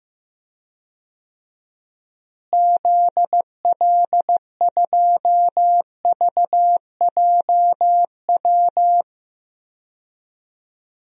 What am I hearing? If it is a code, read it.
Morse code, heard as a keyed tone, about 15 wpm, 700 Hz. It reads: ZL2VJW
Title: DXCC-100 Call Signs - 6 Characters - Repeated 3x - 15wpm